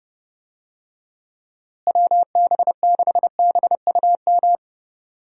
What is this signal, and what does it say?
Morse code, heard as a keyed tone, about 30 wpm, 700 Hz. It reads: WB6BUM
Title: USA Call Signs - 6 Characters - Repeated 2x - 30wpm